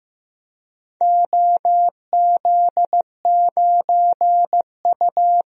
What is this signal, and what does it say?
Morse code, heard as a keyed tone, about 15 wpm, 700 Hz. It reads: OZ9U
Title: DXCC-100 Call Signs - 4 Characters - Repeated 3x - 15wpm